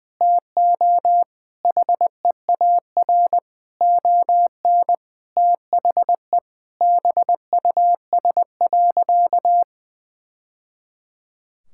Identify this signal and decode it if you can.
Morse code, heard as a keyed tone, about 21 wpm, 700 Hz. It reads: TO HEAR ON THE BUS.
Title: Sentences from Top 1000 Words - Review Part 6 - 20wpm